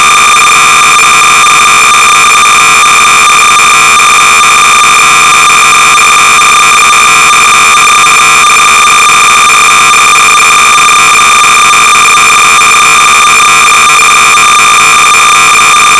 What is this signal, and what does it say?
Cassette data (Kansas City type).
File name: sorcerer / 300baud / Hangman-(Unknown Date and Author)-BASIC-Sorcerer-300baud